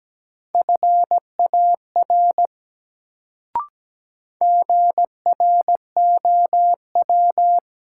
Morse code, heard as a keyed tone, about 17 wpm, 700 Hz. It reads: FAR GROW
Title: Sentences from Top 300 Words - New Word Drill - 17wpm